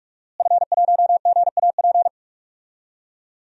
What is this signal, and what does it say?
Morse code, heard as a keyed tone, about 46 wpm, 700 Hz. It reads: F1GAP
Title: DXCC-100 Call Signs - 5 Characters - Repeated 3x - 45wpm